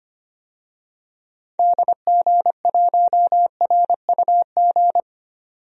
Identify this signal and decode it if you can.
Morse code, heard as a keyed tone, about 25 wpm, 700 Hz. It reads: DG1RUG
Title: DXCC-100 Call Signs - 6 Characters - 25wpm